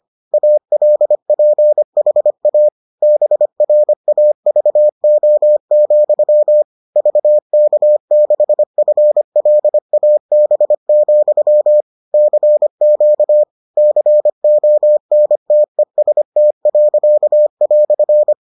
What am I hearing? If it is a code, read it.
Morse code, heard as a keyed tone, about 25 wpm, 600 Hz. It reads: ALPHA BRAVO, VK6FLAB, CQ CONTEST."